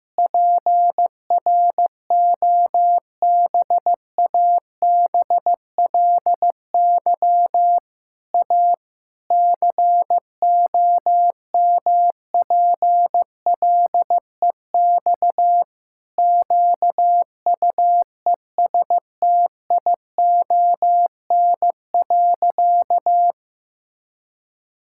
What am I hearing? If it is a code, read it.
Morse code, heard as a keyed tone, about 16 wpm, 700 Hz. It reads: PROBABLY A COMPLEX QUESTION.
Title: Sentences from Top 1000 Words - Review Part 6 - 15wpm